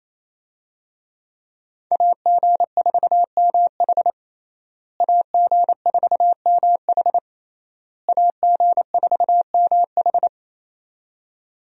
Morse code, heard as a keyed tone, about 27 wpm, 700 Hz. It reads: AG4MH AG4MH AG4MH
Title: USA Call Signs - 5 Characters - Repeated 3x - 28wpm